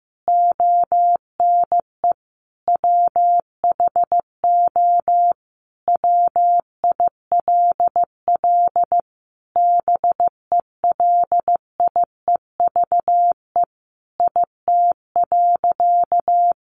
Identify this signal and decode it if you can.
Morse code, heard as a keyed tone, about 15 wpm, 700 Hz. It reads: ONE WHO WILL BELIEVE IT.
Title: Sentences from Top 1000 Words 15wpm